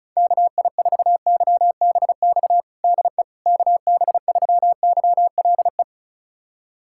Morse code, heard as a keyed tone, about 36 wpm, 700 Hz. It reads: KI4YBX DE KB3YLE
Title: USA Callsign DE Callsign - 6 Characters - 35wpm